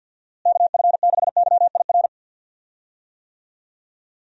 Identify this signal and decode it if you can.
Morse code, heard as a keyed tone, about 52 wpm, 700 Hz. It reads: KF6YIR
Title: USA Call Signs - 6 Characters - Repeated 2x - 50wpm